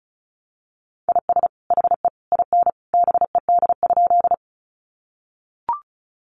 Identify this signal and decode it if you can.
Morse code, heard as a keyed tone, about 34 wpm, 700 Hz. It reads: IS HE IN BED?
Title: Sentences from Top 500 Words 35wpm